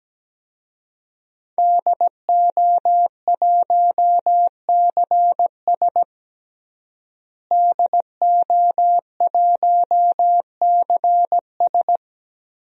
Morse code, heard as a keyed tone, about 17 wpm, 700 Hz. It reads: DO1CS DO1CS
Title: DXCC-100 Call Signs - 5 Characters - Repeated 3x - 17wpm